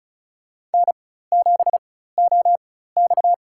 Morse code, heard as a keyed tone, about 36 wpm, spaced as for 16 wpm, 700 Hz. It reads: N7OX
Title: USA Call Signs - 4x Character Spacing - 35wpm